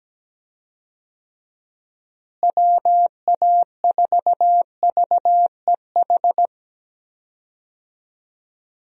Morse code, heard as a keyed tone, about 17 wpm, 700 Hz. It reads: WA4VEH
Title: USA Call Signs - 6 Characters - Repeated 2x - 17wpm